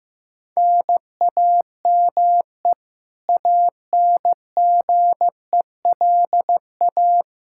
Morse code, heard as a keyed tone, about 16 wpm, 700 Hz. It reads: NAME ANGELA
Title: Practice QSO - Exchange 1 - Beginning - 15wpm